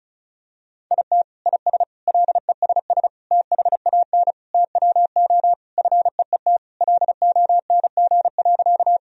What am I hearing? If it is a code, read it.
Morse code, heard as a keyed tone, about 34 wpm, 700 Hz. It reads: IT IS LESS THAN TWO FEET LONG.
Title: Sentences from Top 1000 Words - Review Part 2 - 35wpm